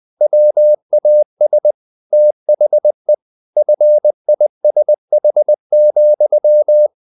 Morse code, heard as a keyed tone, about 20 wpm, 600 Hz. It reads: WAS THE FISH,
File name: the old man and the sea / the old man and the sea part 1 20wpm